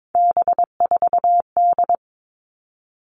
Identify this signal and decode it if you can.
Morse code, heard as a keyed tone, about 22 wpm, 700 Hz. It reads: B4D
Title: All Cut Numbers - 3-Digit 22wpm